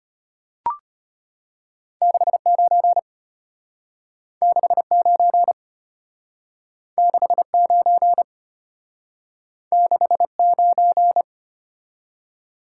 Morse code, heard as a keyed tone, about 32 wpm, 700 Hz. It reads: 69 69 69 69